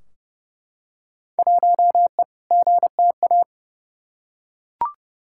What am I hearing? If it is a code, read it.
Morse code, heard as a keyed tone, about 31 wpm, 700 Hz. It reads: 1E GTA
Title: Field Day Exchanges - 30wpm